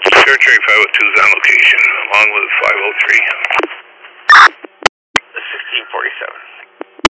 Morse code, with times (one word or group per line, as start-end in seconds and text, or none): none